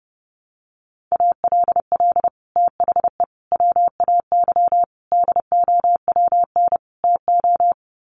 Morse code, heard as a keyed tone, about 30 wpm, 700 Hz. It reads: ALL THE WAY DOWN TO
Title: Sets of 5 Words Encore2 30wpm